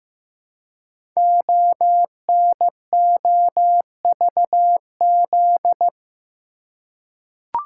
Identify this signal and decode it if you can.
Morse code, heard as a keyed tone, about 15 wpm, 700 Hz. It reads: ONOVZ
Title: Code Groups - 5 Letters - 15wpm